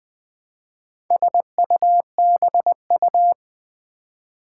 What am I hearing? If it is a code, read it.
Morse code, heard as a keyed tone, about 20 wpm, 700 Hz. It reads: SUBU
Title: Build A Word - 2 Letters At A Time - 8-Letter Words - 20wpm